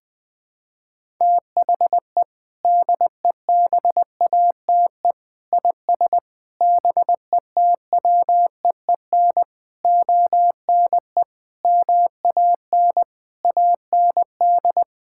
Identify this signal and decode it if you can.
Morse code, heard as a keyed tone, about 21 wpm, 700 Hz. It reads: THE DEBATE IS BETWEEN ONE MAN AND
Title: Sentences with 6-Word Phrases - 20wpm